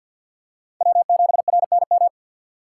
Morse code, heard as a keyed tone, about 48 wpm, 700 Hz. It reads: W7RNM
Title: USA Call Signs - 5 Characters - 6x Character Spacing - 50wpm